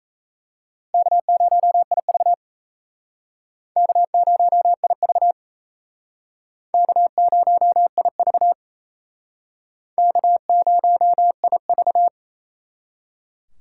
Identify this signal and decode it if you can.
Morse code, heard as a keyed tone, about 35 wpm, 700 Hz. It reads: K0IV K0IV K0IV K0IV
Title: Speed Racing - USA 1x2 Call Signs 28wpm